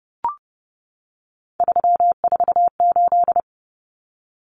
Morse code, heard as a keyed tone, about 30 wpm, 700 Hz. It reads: 348
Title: Common Cut Numbers - 3-Digit 30wpm